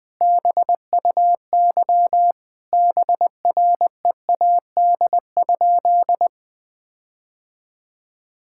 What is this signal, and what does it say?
Morse code, heard as a keyed tone, about 21 wpm, 700 Hz. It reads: BUY BREAD?
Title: Sentences from Top 1600 Words 20wpm